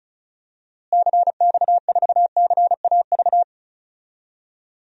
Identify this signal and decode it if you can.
Morse code, heard as a keyed tone, about 34 wpm, 700 Hz. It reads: CX4CAV
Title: DXCC-100 Call Signs - 6 Characters - Repeated 3x - 35wpm